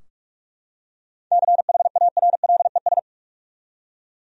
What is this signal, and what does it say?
Morse code, heard as a keyed tone, about 46 wpm, 700 Hz. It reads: CHARLES
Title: USA Common Names - 45wpm